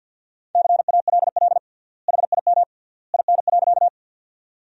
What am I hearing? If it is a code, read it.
Morse code, heard as a keyed tone, about 52 wpm, 700 Hz. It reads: CALL HIM IN.
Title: 3x Word-Spacing - Sentences from Top 100 Words - 50wpm